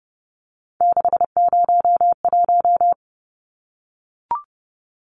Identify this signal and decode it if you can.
Morse code, heard as a keyed tone, about 30 wpm, 700 Hz. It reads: 601
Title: Numbers - 3-Digit 30wpm